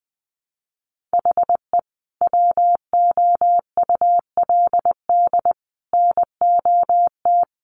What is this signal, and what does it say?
Morse code, heard as a keyed tone, about 20 wpm, 700 Hz. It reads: HE WOULD NOT
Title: Sentences from Top 1300 Words 20wpm